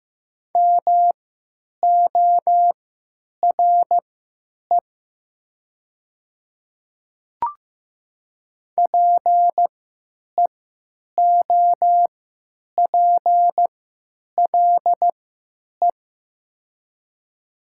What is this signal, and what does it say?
Morse code, heard as a keyed tone, about 15 wpm, spaced as for 9 wpm, 700 Hz. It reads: MORE PEOPLE
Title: 3x Character Spacing - Top 200 Words - 15wpm